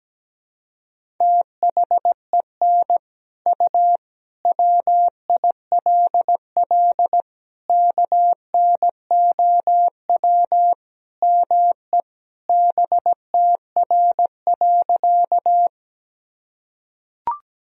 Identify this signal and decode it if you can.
Morse code, heard as a keyed tone, about 17 wpm, 700 Hz. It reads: THEN U WILL KNOW ME BTR.
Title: Sentences from Top 400 Words and Abbreviations 17wpm